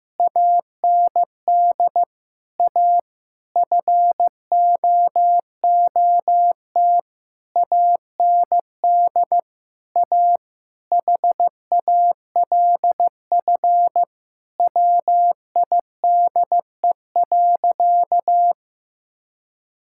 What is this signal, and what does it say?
Morse code, heard as a keyed tone, about 15 wpm, 700 Hz. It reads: AND A FOOT AND A HALF WIDE.
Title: Sentences from Top 1000 Words - Review Part 6 - 15wpm